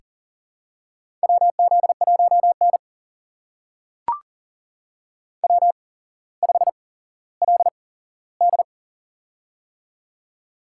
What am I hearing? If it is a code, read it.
Morse code, heard as a keyed tone, about 40 wpm, spaced as for 17 wpm, 700 Hz. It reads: WZ1N W5LD
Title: USA Call Signs - 4 Characters - 8x Character Spacing - 40wpm